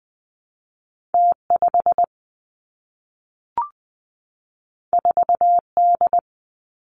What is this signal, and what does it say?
Morse code, heard as a keyed tone, about 20 wpm, 700 Hz. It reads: T 5 4 D
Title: Two Challenging Letter-Number Combinations 20wpm